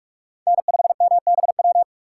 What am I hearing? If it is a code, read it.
Morse code, heard as a keyed tone, about 46 wpm, 700 Hz. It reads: N5MBW
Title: USA Call Signs - 5 Characters - 3x Character Spacing - 45wpm